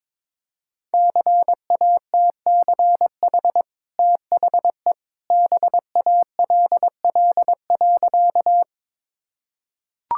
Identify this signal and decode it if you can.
Morse code, heard as a keyed tone, about 22 wpm, 700 Hz. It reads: CATCH THE BALL.
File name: practice / 22wpm / Easy Sentences with Hints - 22wpm